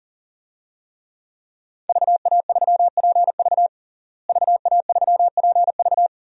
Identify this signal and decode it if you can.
Morse code, heard as a keyed tone, about 37 wpm, 700 Hz. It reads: VA3PV VA3PV
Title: DXCC-100 Call Signs - 5 Characters - Repeated 3x - 40wpm